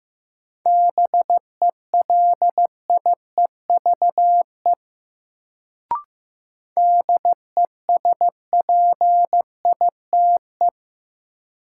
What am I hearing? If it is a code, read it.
Morse code, heard as a keyed tone, about 15 wpm, 700 Hz. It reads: BELIEVE DESPITE
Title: Sets of 4 Words - New Word Drill - 15wpm